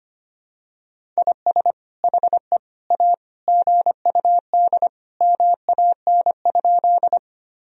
Morse code, heard as a keyed tone, about 25 wpm, 700 Hz. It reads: IS HE A GUD MAN?
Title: Sentences from Top 200 Words and Abbreviations 25wpm